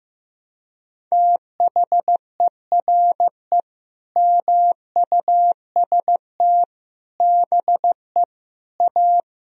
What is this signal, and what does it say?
Morse code, heard as a keyed tone, about 15 wpm, 700 Hz. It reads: THERE MUST BE A
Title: Sentences from Top 1000 Words - Review Part 3 - 15wpm